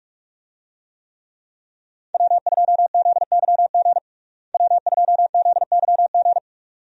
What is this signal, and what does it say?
Morse code, heard as a keyed tone, about 46 wpm, 700 Hz. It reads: W2ZYG W2ZYG
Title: USA Call Signs - 5 Characters - Repeated 2x - 45wpm